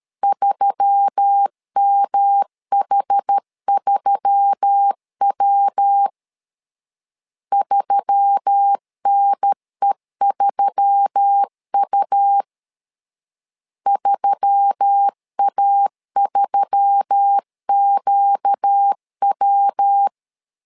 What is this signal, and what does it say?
Morse code, heard as a keyed tone, about 13 wpm, 800 Hz. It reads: 3MH3W 3NE3U 3A3QW
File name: LEZIONI / 4 Q1S3 12wpm 2023